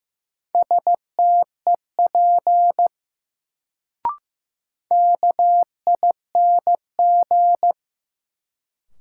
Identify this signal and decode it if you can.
Morse code, heard as a keyed tone, about 15 wpm, 700 Hz. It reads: STEP KING
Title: Sentences from Top 400 Words - New Word Drill - 15wpm